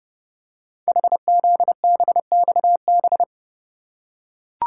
Morse code, heard as a keyed tone, about 30 wpm, 700 Hz. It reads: HZBXB
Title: Code Groups - 5 Letters - 30wpm